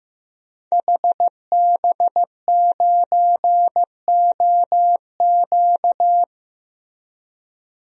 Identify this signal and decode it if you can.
Morse code, heard as a keyed tone, about 15 wpm, 700 Hz. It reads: HB9OQ
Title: DXCC-100 Call Signs - 5 Characters - Repeated 3x - 15wpm